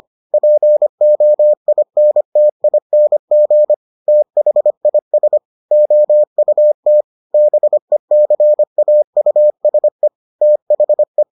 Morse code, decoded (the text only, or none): POINTING THIS OUT BECAUSE THE